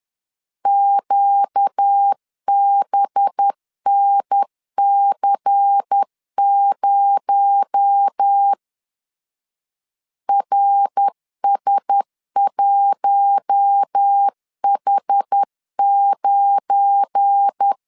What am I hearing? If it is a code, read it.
Morse code, heard as a keyed tone, about 10 wpm, 800 Hz. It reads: QBNC0 RS1H9
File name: LEZIONI / 7 GF9Z / 7 GF9 10wpm 2747